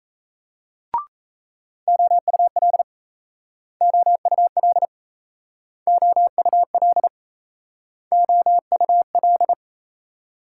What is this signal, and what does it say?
Morse code, heard as a keyed tone, about 37 wpm, 700 Hz. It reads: OUL OUL OUL OUL